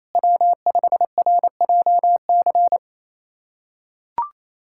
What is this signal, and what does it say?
Morse code, heard as a keyed tone, about 28 wpm, 700 Hz. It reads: W5RJC
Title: USA Callsign DE Callsign - 5 Characters - 28wpm